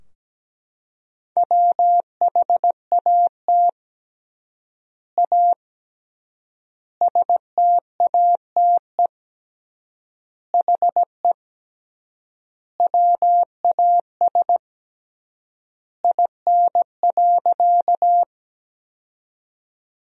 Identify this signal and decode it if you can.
Morse code, heard as a keyed tone, about 17 wpm, 700 Hz. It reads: WHAT A STATE HE WAS IN.
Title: 3x Word-Spacing - Sentences from Top 400 Words - 17wpm